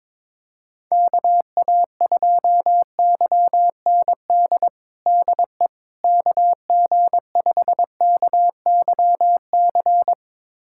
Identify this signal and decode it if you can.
Morse code, heard as a keyed tone, about 22 wpm, 700 Hz. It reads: KA2YND DE KG5KYC
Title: USA Callsign DE Callsign - 6 Characters - 22wpm